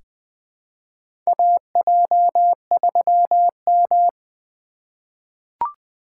Morse code, heard as a keyed tone, about 20 wpm, 700 Hz. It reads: AJ3M
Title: USA Call Signs - 4 Characters - 3x Character Spacing - 20wpm